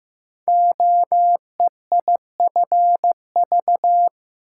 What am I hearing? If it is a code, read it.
Morse code, heard as a keyed tone, about 15 wpm, 700 Hz. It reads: OEIFV
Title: Code Groups - 5 Letters - 15wpm